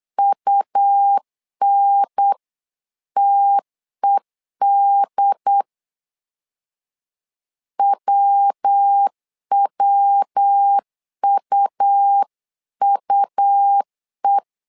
Morse code, heard as a keyed tone, about 8 wpm, 800 Hz. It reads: UNTED WWUUE
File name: LEZIONI / 3 MWRO / 3 MW 8wpm 2501